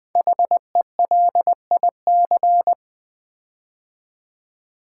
Morse code, heard as a keyed tone, about 20 wpm, 700 Hz. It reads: HELIC
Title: Build A Word - 5 Letters At A Time - 10-Letter Words - 20wpm